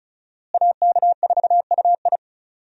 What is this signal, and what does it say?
Morse code, heard as a keyed tone, about 36 wpm, 700 Hz. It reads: AK4UI
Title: USA Call Signs - 5 Characters - Repeated 3x - 35wpm